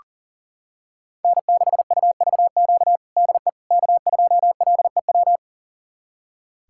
Morse code, heard as a keyed tone, about 40 wpm, 700 Hz. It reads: N6UVQ DE K2LEW
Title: USA Callsign DE Callsign - 5 Characters - 40wpm